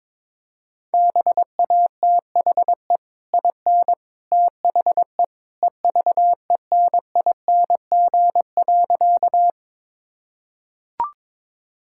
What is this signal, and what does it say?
Morse code, heard as a keyed tone, about 21 wpm, 700 Hz. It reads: BATHE IN THE EVENING.
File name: practice / 22wpm / Easy Sentences with Hints - 22wpm